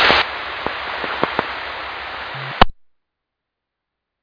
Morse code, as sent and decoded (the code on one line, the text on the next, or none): none